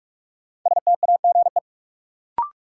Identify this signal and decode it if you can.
Morse code, heard as a keyed tone, about 46 wpm, 700 Hz. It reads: STAGE
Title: Sentences from Top 1200 Words - New Word Drill Alt - 45wpm